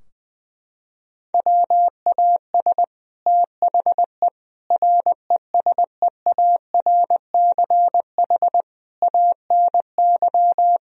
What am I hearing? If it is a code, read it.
Morse code, heard as a keyed tone, about 20 wpm, 700 Hz. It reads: WAS THE RESEARCH ANY
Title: Sentences from Top 1000 Words - Review Part 4 - 20wpm